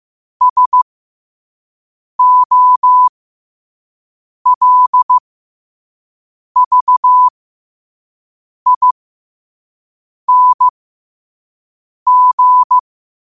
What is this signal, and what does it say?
Morse code, heard as a keyed tone, about 15 wpm, spaced as for 5 wpm, 1000 Hz. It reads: SOLVING